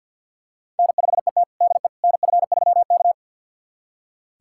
Morse code, heard as a keyed tone, about 47 wpm, 700 Hz. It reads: N5ET DE NF3K
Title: USA Callsign DE Callsign - 4 Characters - 50wpm